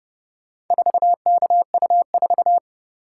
Morse code, heard as a keyed tone, about 30 wpm, 700 Hz. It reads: 4KU4